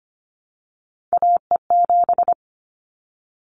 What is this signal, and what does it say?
Morse code, heard as a keyed tone, about 25 wpm, 700 Hz. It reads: AE7
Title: Callsign Prefixes for Top 25 DXCC - 25wpm